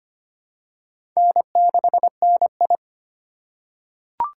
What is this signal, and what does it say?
Morse code, heard as a keyed tone, about 25 wpm, 700 Hz. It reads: N6NI
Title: USA Call Signs - 4 Characters - 4x Character Spacing - 25wpm